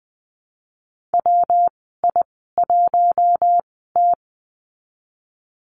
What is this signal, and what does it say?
Morse code, heard as a keyed tone, about 20 wpm, spaced as for 14 wpm, 700 Hz. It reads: WI1T
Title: 147 Lesson 24g - Letter y - Call Signs 20wpm